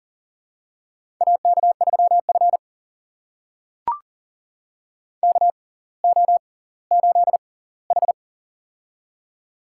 Morse code, heard as a keyed tone, about 40 wpm, spaced as for 20 wpm, 700 Hz. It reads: AK3F KO8H
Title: USA Call Signs - 6x Character Spacing - 40wpm